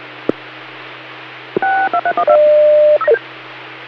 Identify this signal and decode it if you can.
DTMF, heard as telephone touch tones, tones of about 80 ms. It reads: 6231